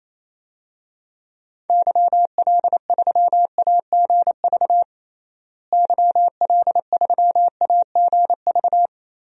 Morse code, heard as a keyed tone, about 28 wpm, 700 Hz. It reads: YL3AGV YL3AGV
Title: DXCC-100 Call Signs - 6 Characters - Repeated 2x - 28wpm